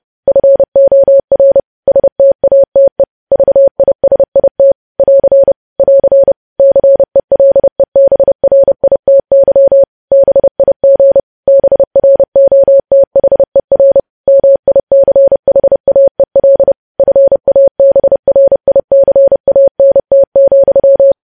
Morse code, heard as a keyed tone, about 30 wpm, 550 Hz. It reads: FOR STATE VISIT <AR> <AR> CELEBRITY BIG BROTHER MICHAEL FABRICANT,
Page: news headlines in morse code at 30 wpm